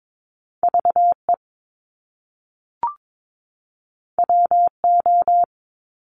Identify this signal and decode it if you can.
Morse code, heard as a keyed tone, about 22 wpm, 700 Hz. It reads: V E W O
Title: Top Two Letter Combinations in Words 22wpm